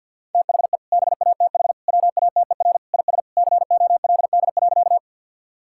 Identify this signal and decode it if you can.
Morse code, heard as a keyed tone, about 50 wpm, 700 Hz. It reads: THE BATH WATER IS COLD.
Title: Sentences from Top 1600 Words 50wpm